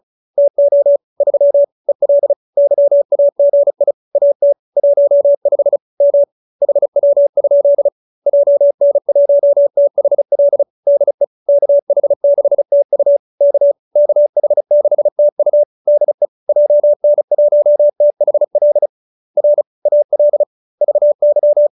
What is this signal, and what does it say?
Morse code, heard as a keyed tone, about 34 wpm, 570 Hz. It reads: TO 3 EL YAGI AT 15 M HW? JN1THL DE KH6TU K KH6TU DE JN1THL R AL VY